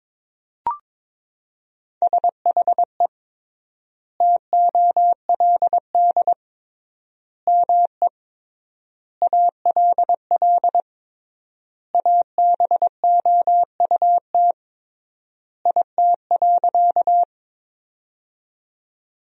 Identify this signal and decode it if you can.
Morse code, heard as a keyed tone, about 22 wpm, 700 Hz. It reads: SHE TOLD ME ALL ABOUT IT.
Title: 3x Word-Spacing - Sentences from Top 400 Words - 22wpm